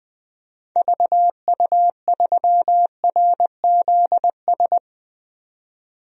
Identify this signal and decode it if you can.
Morse code, heard as a keyed tone, about 20 wpm, 700 Hz. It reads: VU3RZS
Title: DXCC-100 Call Signs - 6 Characters - 20wpm